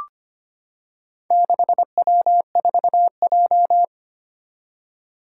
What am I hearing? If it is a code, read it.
Morse code, heard as a keyed tone, about 25 wpm, 700 Hz. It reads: 6W4J
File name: practice / 25wpm / Four Letter-Number Combinations - 25wpm